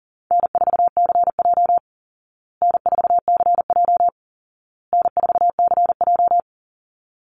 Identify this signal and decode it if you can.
Morse code, heard as a keyed tone, about 40 wpm, 700 Hz. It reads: N4CJ N4CJ N4CJ
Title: Easy Call Signs Repeated 3x - 40wpm